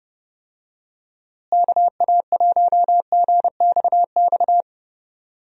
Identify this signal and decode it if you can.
Morse code, heard as a keyed tone, about 30 wpm, 700 Hz. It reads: KA1GXX
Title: USA Call Signs - 6 Characters - Repeated 2x - 30wpm